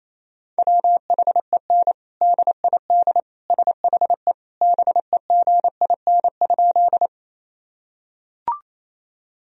Morse code, heard as a keyed tone, about 28 wpm, 700 Hz. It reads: WHEN DID SHE BEGIN?
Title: Sentences from Top 1000 Words - Review Part 1 - 28wpm